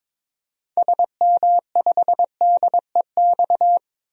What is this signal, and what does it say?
Morse code, heard as a keyed tone, about 23 wpm, 700 Hz. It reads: SM5DEX
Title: DXCC-100 Call Signs - 6 Characters - 22wpm